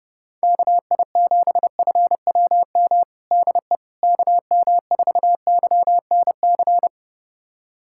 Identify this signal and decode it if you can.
Morse code, heard as a keyed tone, about 32 wpm, 700 Hz. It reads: KI7FWM DE KM4YNC
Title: USA Callsign DE Callsign - 6 Characters - 30wpm